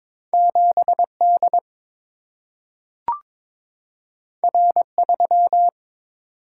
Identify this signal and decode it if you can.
Morse code, heard as a keyed tone, about 22 wpm, 700 Hz. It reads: 7 D R 3
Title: Single-Letter Single-Number 22wpm